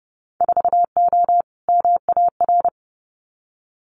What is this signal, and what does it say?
Morse code, heard as a keyed tone, about 30 wpm, 700 Hz. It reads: 4O MAR
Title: Winter Field Day Exchanges Encore 30wpm